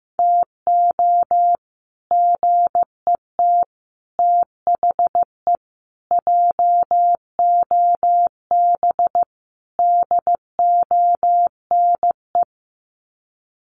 Morse code, heard as a keyed tone, about 15 wpm, 700 Hz. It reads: TO GET THE JOB DONE
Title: Sets of 5 Words Encore2 15wpm